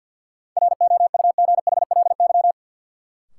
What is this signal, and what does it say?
Morse code, heard as a keyed tone, about 53 wpm, 700 Hz. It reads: ROUGHLY
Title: Sentences from Top 1500 Words - New Word Drill Alt - 50wpm